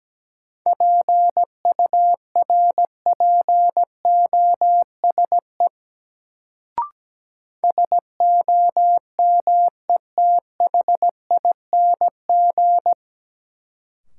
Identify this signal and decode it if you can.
Morse code, heard as a keyed tone, about 17 wpm, 700 Hz. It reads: PURPOSE SOMETHING
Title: Sets of 5 Words - New Word Drill - 17wpm